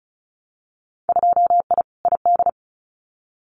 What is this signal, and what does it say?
Morse code, heard as a keyed tone, about 36 wpm, 700 Hz. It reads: 2I ID
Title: Winter Field Day Exchanges Encore 35wpm